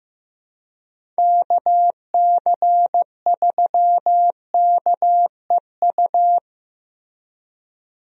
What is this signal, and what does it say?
Morse code, heard as a keyed tone, about 15 wpm, 700 Hz. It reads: KC3KEU
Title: USA Call Signs - 6 Characters - Repeated 3x - 15wpm